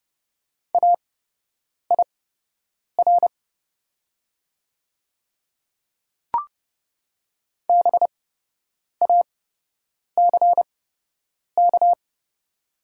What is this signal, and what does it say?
Morse code, heard as a keyed tone, about 30 wpm, spaced as for 8 wpm, 700 Hz. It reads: AIR BACK
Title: 8x Character Spacing - Top 500 Words - 30wpm